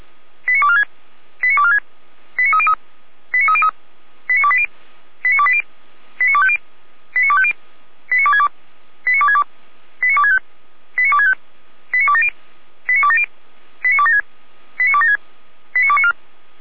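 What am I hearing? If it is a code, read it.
Five-tone selective call (ZVEI), heard as quick runs of five tones, tones of about 70 ms. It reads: 89246 89246 89303 89303 89280 89280 89250 89250 89262 89262 89265 89265 89270 89270 89276 89276 89294
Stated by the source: ZVEI five-tone selective call received by SDR.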